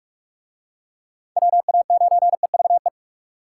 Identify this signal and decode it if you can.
Morse code, heard as a keyed tone, about 42 wpm, 700 Hz. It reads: WA9EVE